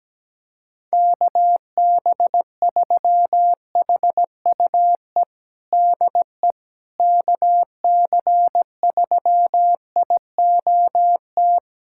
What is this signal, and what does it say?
Morse code, heard as a keyed tone, about 17 wpm, 700 Hz. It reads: KB3HUE DE KC3IOT
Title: USA Callsign DE Callsign - 6 Characters - 17wpm